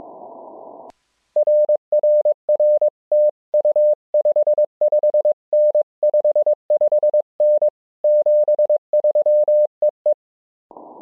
Morse code, heard as a keyed tone, about 22 wpm, 600 Hz. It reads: RRR TU 55N 55N 73 EE